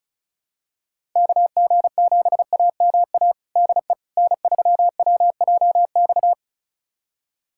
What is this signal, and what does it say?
Morse code, heard as a keyed tone, about 34 wpm, 700 Hz. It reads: KG7AMA DE N3WJX
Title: USA Callsign DE Callsign 35wpm